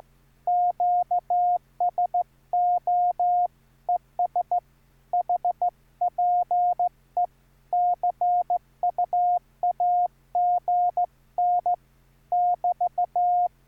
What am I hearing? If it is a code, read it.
Morse code, heard as a keyed tone, about 14 wpm, 710 Hz. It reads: QSO ES HPE CUAGN <BT>